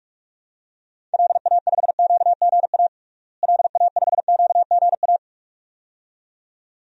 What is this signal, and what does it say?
Morse code, heard as a keyed tone, about 46 wpm, 700 Hz. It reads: LA5QGA LA5QGA
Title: DXCC-100 Call Signs - 6 Characters - Repeated 2x - 45wpm